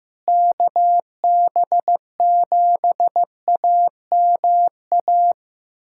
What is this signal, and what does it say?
Morse code, heard as a keyed tone, about 16 wpm, 700 Hz. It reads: KB7AMA
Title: USA Callsign DE Callsign - 6 Characters - 15wpm